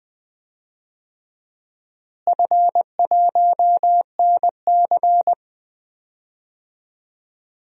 Morse code, heard as a keyed tone, about 20 wpm, 700 Hz. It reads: F1NC
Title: DXCC-100 Call Signs - 4 Characters - Repeated 2x - 20wpm